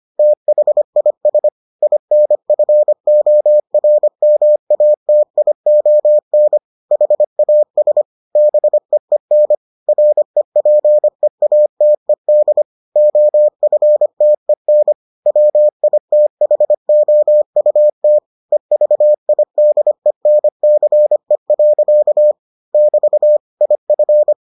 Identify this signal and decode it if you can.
Morse code, heard as a keyed tone, about 25 wpm, 600 Hz. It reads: THIS INFORMATION HAS BEEN REPEATED OFTEN WITHOUT EVIDENCE. <BT> IF